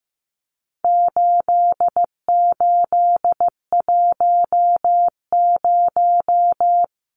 Morse code, heard as a keyed tone, about 15 wpm, 700 Hz. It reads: 8810
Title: Numbers - 4-Digit 15wpm